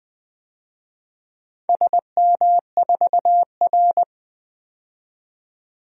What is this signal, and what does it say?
Morse code, heard as a keyed tone, about 20 wpm, 700 Hz. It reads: SM4R
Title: DXCC-100 Call Signs - 4 Characters - 20wpm